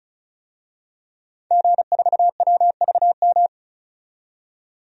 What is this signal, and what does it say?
Morse code, heard as a keyed tone, about 36 wpm, 700 Hz. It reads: G4WVM
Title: DXCC-100 Call Signs - 5 Characters - 35wpm